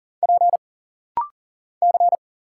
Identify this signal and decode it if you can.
Morse code, heard as a keyed tone, about 40 wpm, 700 Hz. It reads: P C
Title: Single Letters - Rapid-Fire 40wpm